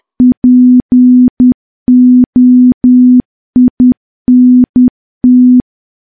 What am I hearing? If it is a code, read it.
Morse code, heard as a keyed tone, about 10 wpm, 250 Hz. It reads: POINT
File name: morse / morse-training-10WPM